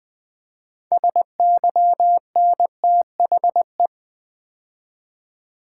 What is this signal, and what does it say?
Morse code, heard as a keyed tone, about 20 wpm, 700 Hz. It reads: SYNTHE
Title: Build A Word - 3 Letters At A Time - 9-Letter Words - 20wpm